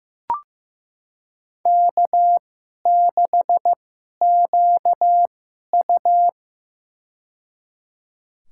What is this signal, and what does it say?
Morse code, heard as a keyed tone, about 15 wpm, spaced as for 11 wpm, 700 Hz. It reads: K6QU
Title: USA Call Signs - 4 Characters - 2x Character Spacing - 15wpm